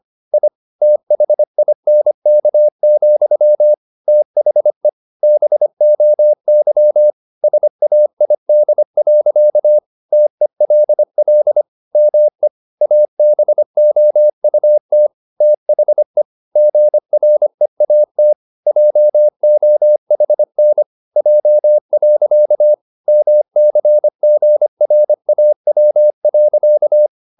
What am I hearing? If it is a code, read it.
Morse code, heard as a keyed tone, about 25 wpm, 600 Hz. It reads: I THINK, THE BOY SAID. TELL ME ABOUT THE GREAT JOHN J. MCGRAW.
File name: the old man and the sea part 1 25wpm